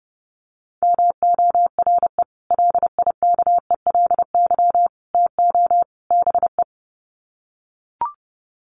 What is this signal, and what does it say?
Morse code, heard as a keyed tone, about 30 wpm, 700 Hz. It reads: MORE LIKELY TO BE
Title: Sets of 4 Words 30wpm